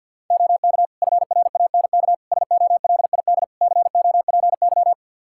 Morse code, heard as a keyed tone, about 45 wpm, 700 Hz. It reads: OK FRANK SOLID COPY
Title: Practice QSO - Exchange 3 - More Info - 50wpm